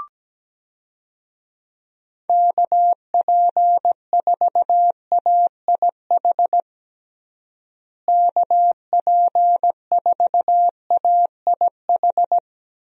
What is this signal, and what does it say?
Morse code, heard as a keyed tone, about 17 wpm, 700 Hz. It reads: KP4AIH KP4AIH
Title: DXCC-100 Call Signs - 6 Characters - Repeated 2x - 17wpm